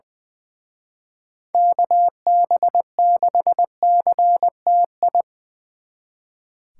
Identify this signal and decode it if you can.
Morse code, heard as a keyed tone, about 24 wpm, 700 Hz. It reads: KB6CTI
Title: Speed Racing - USA 2x3 Call Signs 20wpm